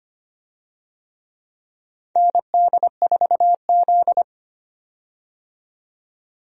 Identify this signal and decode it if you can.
Morse code, heard as a keyed tone, about 25 wpm, 700 Hz. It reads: ND4Z
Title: USA Call Signs - 4 Characters - 25wpm